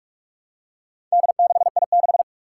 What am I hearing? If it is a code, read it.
Morse code, heard as a keyed tone, about 46 wpm, 700 Hz. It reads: D6I6
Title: Four Challenging Letter-Number Combinations 45wpm